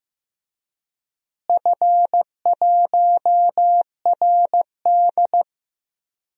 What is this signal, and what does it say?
Morse code, heard as a keyed tone, about 15 wpm, 700 Hz. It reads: F1RD
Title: DXCC-100 Call Signs - 4 Characters - Repeated 2x - 15wpm